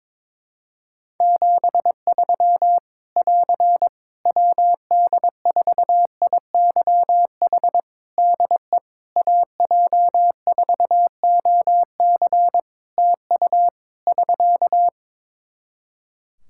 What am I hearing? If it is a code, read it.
Morse code, heard as a keyed tone, about 22 wpm, 700 Hz. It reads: 73 <AR> WD4IYH DE AJ4OC TU <SK>
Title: Practice QSO - Exchange 4 - Ending - 22wpm